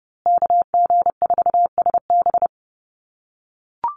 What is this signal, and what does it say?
Morse code, heard as a keyed tone, about 30 wpm, 700 Hz. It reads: KG4SB
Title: USA Call Signs Repeated 3x 30wpm